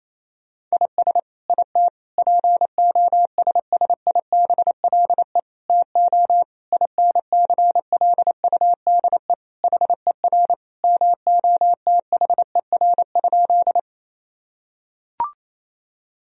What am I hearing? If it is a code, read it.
Morse code, heard as a keyed tone, about 28 wpm, 700 Hz. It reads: IS IT POSSIBLE TO INCLUDE HER MOTHER?
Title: Sentences from Top 1000 Words - Review Part 3 - 28wpm